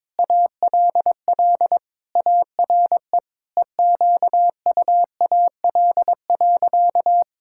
Morse code, heard as a keyed tone, about 22 wpm, 700 Hz. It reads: ALL ARE EQUAL.
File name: Sentences with 9-Word Phrases - 22wpm